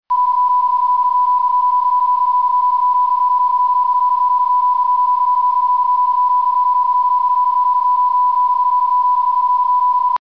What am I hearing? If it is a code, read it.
PSK31, about 31.25 baud.